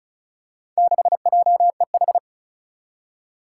Morse code, heard as a keyed tone, about 34 wpm, 700 Hz. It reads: 6JEH
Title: Four Challenging Letter-Number Combinations 35wpm